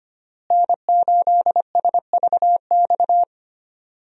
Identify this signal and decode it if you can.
Morse code, heard as a keyed tone, about 25 wpm, 700 Hz. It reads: N8SVX
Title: USA Call Signs - 5 Characters - Repeated 2x - 25wpm